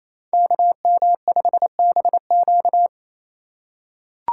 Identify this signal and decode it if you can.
Morse code, heard as a keyed tone, about 28 wpm, 700 Hz. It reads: KM5BQ
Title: USA Call Signs - 5 Characters - 4x Character Spacing - 28wpm